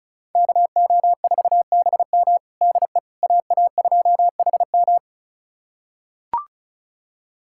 Morse code, heard as a keyed tone, about 36 wpm, 700 Hz. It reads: KO4BM DE AA2HM
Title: USA Callsign DE Callsign - 5 Characters - 35wpm